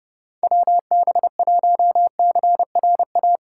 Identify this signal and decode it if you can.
Morse code, heard as a keyed tone, about 30 wpm, 700 Hz. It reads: WB1CRA
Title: USA Call Signs - 4x Character Spacing - 30wpm